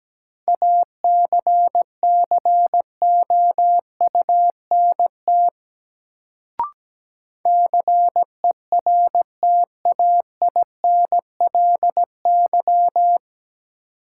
Morse code, heard as a keyed tone, about 17 wpm, 700 Hz. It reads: ACCOUNT CERTAINLY
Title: Sentences from Top 900 Words - New Word Drill - 17wpm